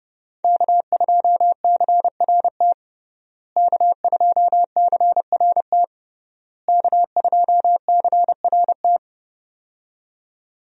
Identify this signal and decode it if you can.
Morse code, heard as a keyed tone, about 30 wpm, 700 Hz. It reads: K2CRT K2CRT K2CRT
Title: USA Call Signs - 5 Characters - Repeated 3x - 30wpm